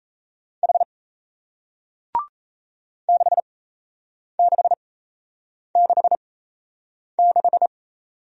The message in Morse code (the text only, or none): H 6666